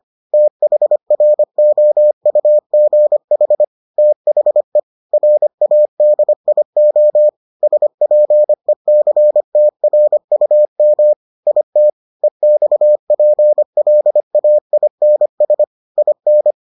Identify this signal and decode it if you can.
Morse code, heard as a keyed tone, about 25 wpm, 600 Hz. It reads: THROUGH THE RADIO SPECTRUM IT EXPLAINS IN